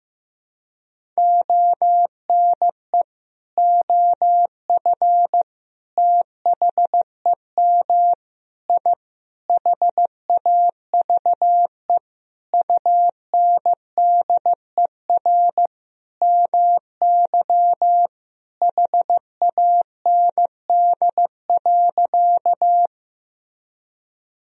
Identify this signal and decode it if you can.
Morse code, heard as a keyed tone, about 15 wpm, 700 Hz. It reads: ONE OF THEM I HAVE UNDER MY HAND.
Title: Sentences from Top 1000 Words - Review Part 1 - 15wpm